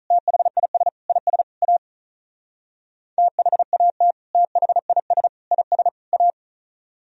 Morse code, heard as a keyed tone, about 43 wpm, 700 Hz. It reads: THIS IS A THAT THIS IS A
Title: Speed-Racing - Sets of 4 Words 35wpm